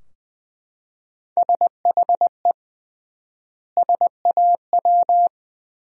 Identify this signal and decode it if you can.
Morse code, heard as a keyed tone, about 20 wpm, 700 Hz. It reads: SHE SAW
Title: 3x Word-Spacing - Sentences from Top 300 Words - 20wpm